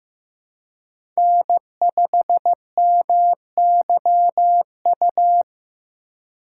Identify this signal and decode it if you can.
Morse code, heard as a keyed tone, about 15 wpm, 700 Hz. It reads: N5MYU
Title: USA Call Signs - 5 Characters - 3x Character Spacing - 15wpm